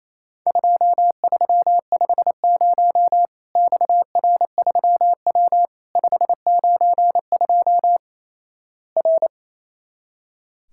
Morse code, heard as a keyed tone, about 28 wpm, 700 Hz. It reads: 2350 XR3W 592 R
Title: Worked All Europe DX Contest - QTC Exchange - 28wpm